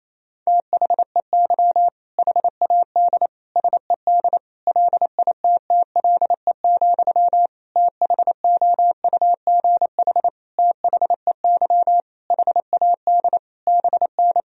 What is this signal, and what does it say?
Morse code, heard as a keyed tone, about 28 wpm, 700 Hz. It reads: THEY HAD SED LITTLE, THOUGH THEY HAD BN
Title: Sentences from Top 400 Words and Abbreviations 28wpm